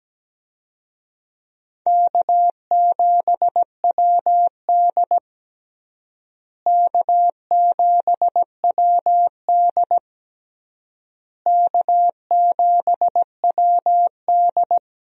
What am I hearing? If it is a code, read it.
Morse code, heard as a keyed tone, about 17 wpm, 700 Hz. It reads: K7WD K7WD K7WD
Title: USA Call Signs - 4 Characters - Repeated 3x - 17wpm